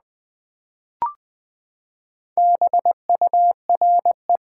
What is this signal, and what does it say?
Morse code, heard as a keyed tone, about 20 wpm, 700 Hz. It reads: BURE
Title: Build A Word - 4 Letters At A Time - 11-Letter Words - 20wpm